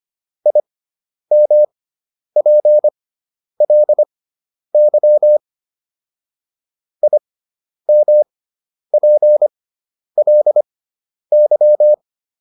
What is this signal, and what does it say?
Morse code, heard as a keyed tone, about 25 wpm, spaced as for 10 wpm, 600 Hz. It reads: IMPLY IMPLY